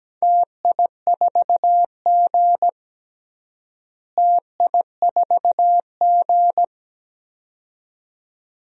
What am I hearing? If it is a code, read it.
Morse code, heard as a keyed tone, about 17 wpm, 700 Hz. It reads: TI4G TI4G
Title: DXCC-100 Call Signs - 4 Characters - Repeated 2x - 17wpm